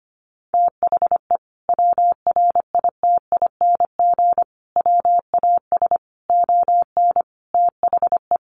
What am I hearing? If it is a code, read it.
Morse code, heard as a keyed tone, about 25 wpm, 700 Hz. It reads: THE WRITING WAS ON THE
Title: Sentences from Top 1100 Words 25wpm